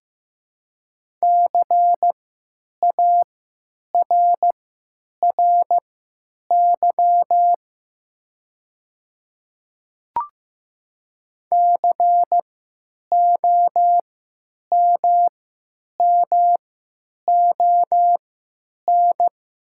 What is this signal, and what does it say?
Morse code, heard as a keyed tone, about 15 wpm, spaced as for 9 wpm, 700 Hz. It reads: CARRY COMMON
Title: 3x Character Spacing - Top 500 Words - 15wpm